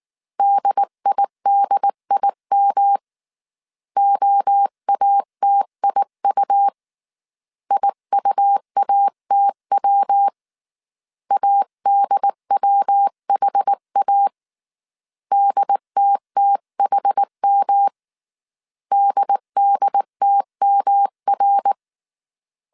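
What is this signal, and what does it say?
Morse code, heard as a keyed tone, about 19 wpm, 800 Hz. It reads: DIDIM OATIU IUATW ADWHA DTTHM DDTMR
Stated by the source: Morse code dictation at 18 wpm from the IW3SGT QRS Morse code training course, five-character groups of 300 characters.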